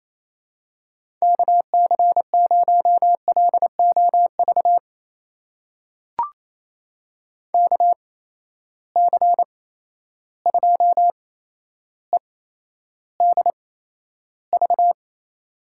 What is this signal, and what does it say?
Morse code, heard as a keyed tone, about 27 wpm, spaced as for 12 wpm, 700 Hz. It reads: KC0LOV KC2EDV
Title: USA Call Signs - 6 Characters - 8x Character Spacing - 28wpm